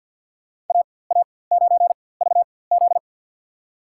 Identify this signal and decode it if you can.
Morse code, heard as a keyed tone, about 53 wpm, spaced as for 24 wpm, 700 Hz. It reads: AA9VZ
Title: USA Call Signs - 5 Characters - 4x Character Spacing - 50wpm